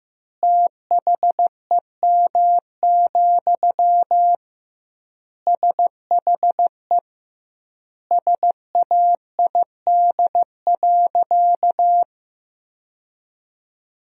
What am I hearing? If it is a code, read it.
Morse code, heard as a keyed tone, about 15 wpm, 700 Hz. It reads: THEM, SHE SAID.
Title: 2x Word-Spacing - Sentences from Top 400 Words - 15wpm